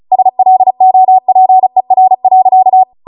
Morse code, heard as a keyed tone, about 36 wpm, 750 Hz. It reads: SLOPER.
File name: Morse / 35WPM